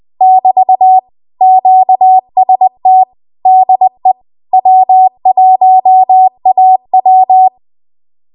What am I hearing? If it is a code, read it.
Morse code, heard as a keyed tone, about 20 wpm, 750 Hz. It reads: <BT> QST DE W1AW
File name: Morse / 20WPM